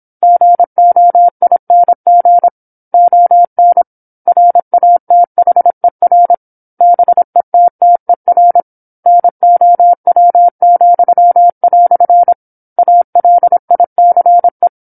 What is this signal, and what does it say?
Morse code, heard as a keyed tone, about 27 wpm, 700 Hz. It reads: GOING ON RATHER BETTER NOW," ALICE
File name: 08-Chapter-08 26 WPM